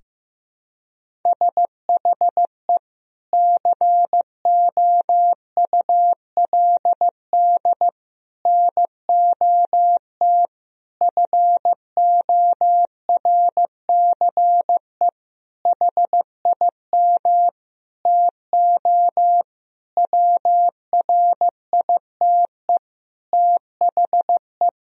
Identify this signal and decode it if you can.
Morse code, heard as a keyed tone, about 15 wpm, 700 Hz. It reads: SHE COULD NOT FORCE HIM TO WRITE THE
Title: Sentences from Top 1000 Words - Review Part 3 - 15wpm